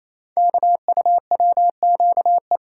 Morse code, heard as a keyed tone, about 28 wpm, 700 Hz. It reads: KUWQE
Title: Code Groups - 5 Letters - 28wpm